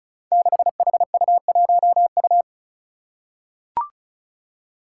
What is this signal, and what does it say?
Morse code, heard as a keyed tone, about 34 wpm, 700 Hz. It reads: 6HU1U
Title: Five Challenging Letter-Number Combinations 35wpm